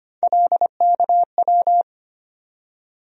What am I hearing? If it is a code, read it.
Morse code, heard as a keyed tone, about 25 wpm, 700 Hz. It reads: LKW
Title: USA Call Sign Postfixes - 3 Letters 25wpm